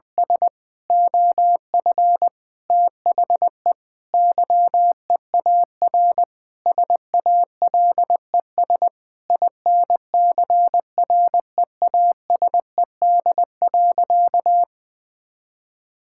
Morse code, heard as a keyed tone, about 20 wpm, 700 Hz. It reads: S OF THE YEAR SALES INCREASED.
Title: Sentences with 8-Word Phrases - 20wpm